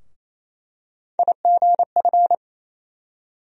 Morse code, heard as a keyed tone, about 28 wpm, 700 Hz. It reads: IGF
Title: USA Call Sign Postfixes - 3 Letters 28wpm